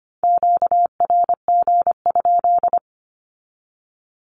Morse code, heard as a keyed tone, about 25 wpm, 700 Hz. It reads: QRG?
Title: Q-Codes 25wpm